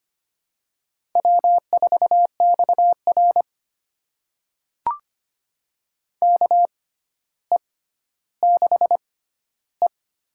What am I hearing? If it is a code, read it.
Morse code, heard as a keyed tone, about 25 wpm, spaced as for 13 wpm, 700 Hz. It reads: W4XR KE6E
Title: USA Call Signs - 4 Characters - 6x Character Spacing - 25wpm